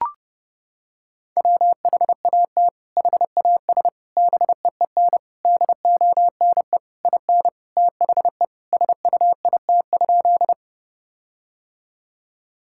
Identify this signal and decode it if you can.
Morse code, heard as a keyed tone, about 32 wpm, 700 Hz. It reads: WHAT HAS BEEN DONE IN THE SUIT?
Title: Sentences from Top 1600 Words 30wpm